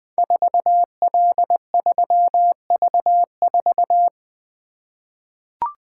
Morse code, heard as a keyed tone, about 20 wpm, 700 Hz. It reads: 4L3V4
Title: Code Groups - 5 Letter:Numbers - 20wpm